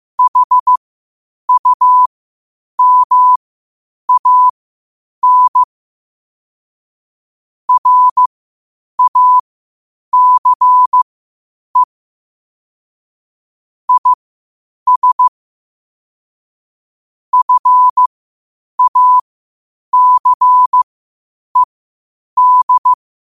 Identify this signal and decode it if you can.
Morse code, heard as a keyed tone, about 15 wpm, spaced as for 8 wpm, 1000 Hz. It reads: HUMAN RACE IS FACED
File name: qotdfiles / 8wpm